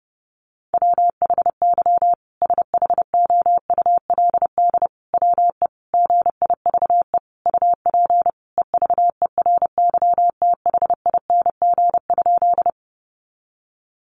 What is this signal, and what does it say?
Morse code, heard as a keyed tone, about 30 wpm, 700 Hz. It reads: WHY SHOULD WE GIVE UP EVERYTHING?
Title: Sentences from Top 700 Words 30wpm